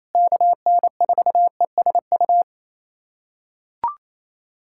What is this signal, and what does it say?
Morse code, heard as a keyed tone, about 27 wpm, 700 Hz. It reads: KN4ESU
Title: USA Callsign DE Callsign 28wpm